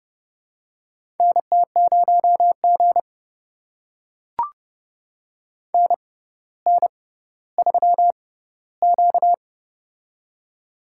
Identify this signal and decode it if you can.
Morse code, heard as a keyed tone, about 30 wpm, spaced as for 15 wpm, 700 Hz. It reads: NT0G NN3Q
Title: USA Call Signs - 4 Characters - 6x Character Spacing - 30wpm